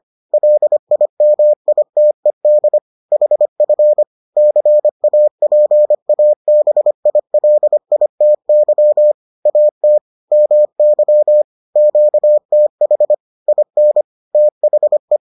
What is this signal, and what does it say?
Morse code, heard as a keyed tone, about 25 wpm, 600 Hz. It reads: LIMITED HF CAPABILITY AT MY QTH IN THE